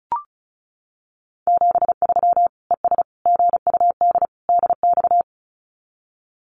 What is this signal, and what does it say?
Morse code, heard as a keyed tone, about 34 wpm, 700 Hz. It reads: 73 ES GUD DX
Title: Common QSO Phrases 35wpm